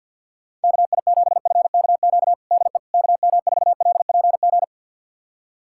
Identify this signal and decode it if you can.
Morse code, heard as a keyed tone, about 53 wpm, 700 Hz. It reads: KI7FKQ DE KM4LPG
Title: USA Callsign DE Callsign - 6 Characters - 50wpm